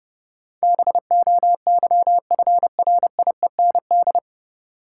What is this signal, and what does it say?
Morse code, heard as a keyed tone, about 32 wpm, 700 Hz. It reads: BOYFRIEND
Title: Sentences from Top 1000 Words - Review Part 5 - New Word Drill - 30wpm